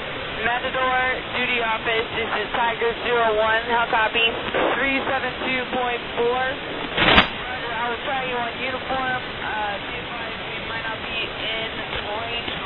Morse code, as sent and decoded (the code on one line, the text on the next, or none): none